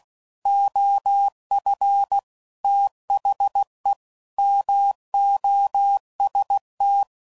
Morse code, heard as a keyed tone, about 16 wpm, 800 Hz. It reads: OF THE MOST